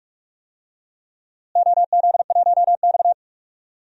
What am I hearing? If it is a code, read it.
Morse code, heard as a keyed tone, about 45 wpm, 700 Hz. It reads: OZ1X
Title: DXCC-100 Call Signs - 4 Characters - Repeated 2x - 45wpm